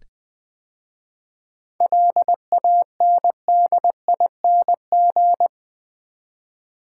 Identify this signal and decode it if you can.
Morse code, heard as a keyed tone, about 20 wpm, 690 Hz. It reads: LANDING
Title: 7 Letter Words - First 3 Letters Given - 20wpm